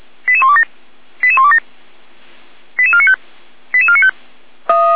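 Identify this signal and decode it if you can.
Five-tone selective call (ZVEI), heard as quick runs of five tones, tones of about 70 ms. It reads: 88127 88127 88495 88495
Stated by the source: ZVEI five-tone selective call 88191, received by SDR.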